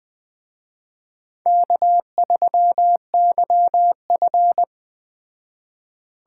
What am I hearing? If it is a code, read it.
Morse code, heard as a keyed tone, about 20 wpm, 700 Hz. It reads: K3YF
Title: Speed Racing - USA 1x2 Call Signs 17wpm